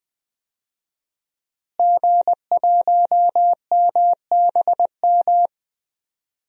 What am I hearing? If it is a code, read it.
Morse code, heard as a keyed tone, about 20 wpm, 700 Hz. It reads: G1MBM
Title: DXCC-100 Call Signs - 5 Characters - Repeated 2x - 20wpm